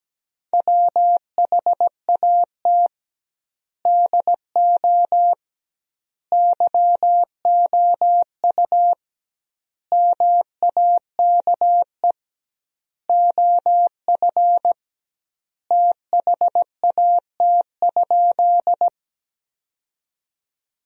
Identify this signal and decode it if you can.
Morse code, heard as a keyed tone, about 17 wpm, 700 Hz. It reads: WHAT DO YOU MAKE OF THAT?
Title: 2x Word-Spacing - Sentences from Top 100 Words - 17wpm